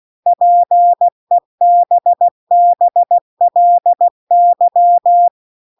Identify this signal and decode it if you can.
Morse code, heard as a keyed tone, about 16 wpm, 700 Hz. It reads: PEBBLY
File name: CW-The-Big-Twoheaded-River-2-16WPM